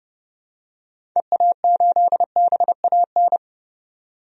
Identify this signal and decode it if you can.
Morse code, heard as a keyed tone, about 30 wpm, 700 Hz. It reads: EA8BAN
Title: DXCC-100 Call Signs - 6 Characters - 30wpm